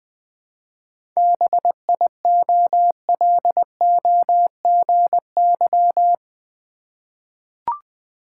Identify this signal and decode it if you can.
Morse code, heard as a keyed tone, about 20 wpm, 690 Hz. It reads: BIOLOGY
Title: 7 Letter Words - First 3 Letters Given - 20wpm